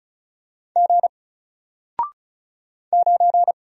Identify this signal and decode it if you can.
Morse code, heard as a keyed tone, about 35 wpm, 700 Hz. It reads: G 9
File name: Single-Letter-Number-35wpm